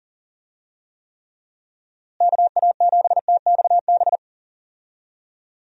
Morse code, heard as a keyed tone, about 40 wpm, 700 Hz. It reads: KA7TXB
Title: USA Call Signs - 6 Characters - Repeated 2x - 40wpm